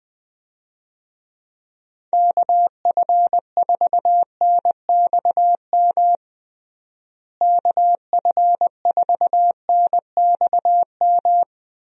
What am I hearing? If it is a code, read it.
Morse code, heard as a keyed tone, about 20 wpm, 700 Hz. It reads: KF4NXM KF4NXM
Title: USA Call Signs - 6 Characters - Repeated 2x - 20wpm